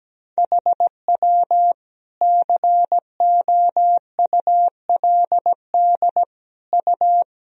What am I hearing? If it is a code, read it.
Morse code, heard as a keyed tone, about 17 wpm, 700 Hz. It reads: HW COULD U
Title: Sentences from Top 300 Words and Abbreviations 17wpm